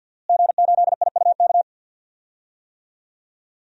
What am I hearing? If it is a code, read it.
Morse code, heard as a keyed tone, about 52 wpm, 700 Hz. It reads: G8IUK
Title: DXCC-100 Call Signs - 5 Characters - Repeated 3x - 50wpm